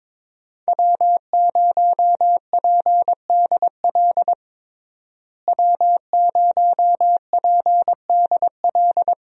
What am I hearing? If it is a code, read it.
Morse code, heard as a keyed tone, about 22 wpm, 700 Hz. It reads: W0PDL W0PDL
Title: USA Call Signs - 5 Characters - Repeated 3x - 22wpm